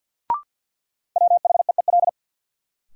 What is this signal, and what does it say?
Morse code, heard as a keyed tone, about 53 wpm, 700 Hz. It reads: WHEEL
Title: Sentences from Top 500 Words - New Word Drill - 50wpm